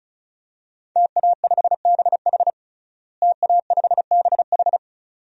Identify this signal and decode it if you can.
Morse code, heard as a keyed tone, about 36 wpm, 700 Hz. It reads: TA5BH TA5BH
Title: DXCC-100 Call Signs - 5 Characters - Repeated 2x - 35wpm